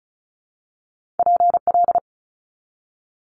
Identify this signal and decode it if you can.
Morse code, heard as a keyed tone, about 36 wpm, 700 Hz. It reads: PL
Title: Multi-letter Phonemes 35wpm